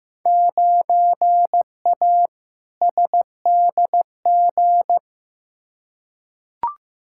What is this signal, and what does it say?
Morse code, heard as a keyed tone, about 15 wpm, 700 Hz. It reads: 9A SDG
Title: Field Day Exchanges Encore - 15wpm